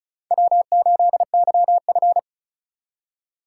Morse code, heard as a keyed tone, about 34 wpm, 700 Hz. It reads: W8YF
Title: USA Call Signs - 4 Characters - 8x Character Spacing - 35wpm